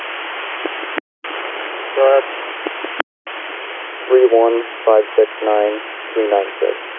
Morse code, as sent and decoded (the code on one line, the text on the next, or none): none